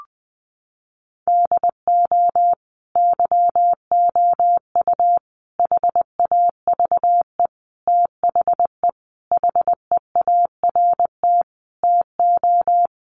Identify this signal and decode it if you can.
Morse code, heard as a keyed tone, about 20 wpm, 700 Hz. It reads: DO YOU HAVE THE HEART TO
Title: Sentences from Top 1200 Words 20wpm